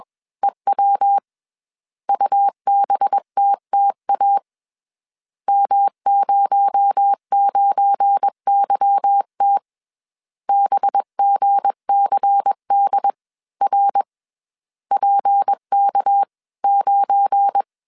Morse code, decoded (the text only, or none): EW UBTTA M09YT BGCDR PK9